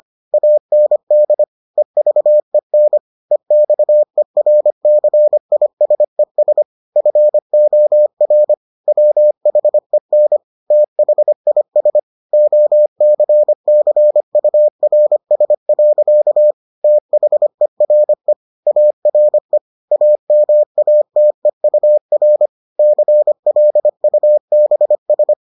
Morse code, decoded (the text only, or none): AND EVEN EXERCISES FOR WHEN THIS OCCURS. THERE ARE AMATEUR CLUBS